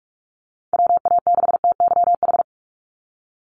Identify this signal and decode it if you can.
Morse code, heard as a keyed tone, about 46 wpm, 700 Hz. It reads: WA6TYH
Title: USA Call Signs 45wpm